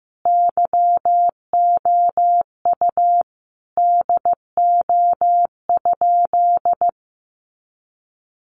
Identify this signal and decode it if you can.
Morse code, heard as a keyed tone, about 15 wpm, 700 Hz. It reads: YOU DO?
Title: Sentences from Top 100 Words 15wpm